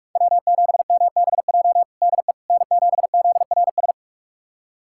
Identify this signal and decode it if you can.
Morse code, heard as a keyed tone, about 47 wpm, 700 Hz. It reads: W7MBJ DE N7ZRS
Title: USA Callsign DE Callsign - 5 Characters - 45wpm